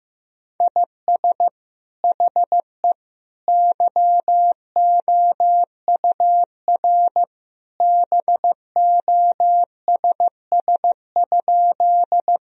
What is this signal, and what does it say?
Morse code, heard as a keyed tone, about 15 wpm, 700 Hz. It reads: IS HE YOUR BOSS?
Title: Sentences from Top 1000 Words - Review Part 4 - 15wpm